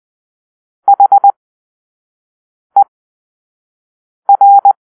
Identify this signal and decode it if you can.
Morse code, heard as a keyed tone, about 20 wpm, spaced as for 5 wpm, 800 Hz. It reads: HER